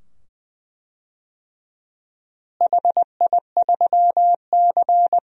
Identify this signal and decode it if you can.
Morse code, heard as a keyed tone, about 20 wpm, 700 Hz. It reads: HI3C
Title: DXCC-100 Call Signs - 4 Characters - Repeated 2x - 20wpm